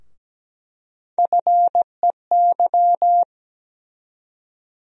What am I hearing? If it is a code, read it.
Morse code, heard as a keyed tone, about 17 wpm, 700 Hz. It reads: FEY
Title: USA Call Sign Postfixes - 3 Letters 17wpm